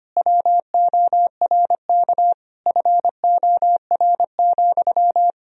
Morse code, heard as a keyed tone, about 25 wpm, 700 Hz. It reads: WORK FOR,
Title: Sentences from Top 1000 Words - Review Part 4 - 25wpm